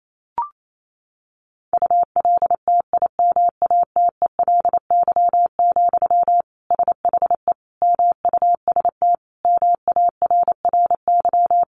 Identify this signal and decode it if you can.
Morse code, heard as a keyed tone, about 28 wpm, 700 Hz. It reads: ULTIMATELY, SHE MUST MARRY
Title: Sentences from Top 1200 Words 28wpm